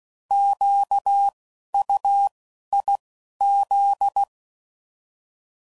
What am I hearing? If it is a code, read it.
Morse code, heard as a keyed tone, about 16 wpm, 800 Hz. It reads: Q U I Z
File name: Prefixes / By First Letter / Qs